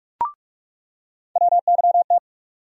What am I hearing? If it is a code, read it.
Morse code, heard as a keyed tone, about 44 wpm, 700 Hz. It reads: WYT
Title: USA Call Sign Postfixes - 3 Letters 45wpm